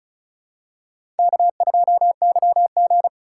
Morse code, heard as a keyed tone, about 36 wpm, 700 Hz. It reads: K2YG
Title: USA Call Signs - 2x Character Spacing - 35wpm